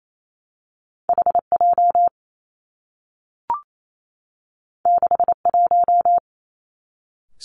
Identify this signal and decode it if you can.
Morse code, heard as a keyed tone, about 28 wpm, 700 Hz. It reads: H J 6 1
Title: Two Challenging Letter-Number Combinations 28wpm